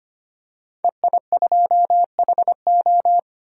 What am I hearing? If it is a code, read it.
Morse code, heard as a keyed tone, about 25 wpm, 700 Hz. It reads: EI2HO
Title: DXCC-100 Call Signs - 5 Characters - 25wpm